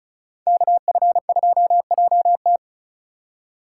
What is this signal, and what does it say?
Morse code, heard as a keyed tone, about 42 wpm, 700 Hz. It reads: KF2JT
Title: Speed Racing - USA 2x2 Call Signs 35wpm